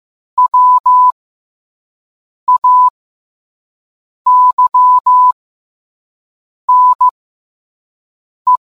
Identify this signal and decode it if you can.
Morse code, heard as a keyed tone, about 15 wpm, spaced as for 5 wpm, 1000 Hz. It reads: WAYNE